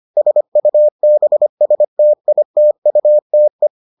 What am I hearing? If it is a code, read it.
Morse code, heard as a keyed tone, about 25 wpm, 600 Hz. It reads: SUBSTITUTE